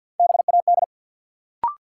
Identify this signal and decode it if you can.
Morse code, heard as a keyed tone, about 52 wpm, 700 Hz. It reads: BAD
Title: Sentences from Top 700 Words - New Word Drill - 50wpm